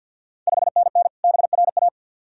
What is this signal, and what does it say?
Morse code, heard as a keyed tone, about 51 wpm, 690 Hz. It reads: 5NN BRA
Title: Ontario QSO Party Exchange 50wpm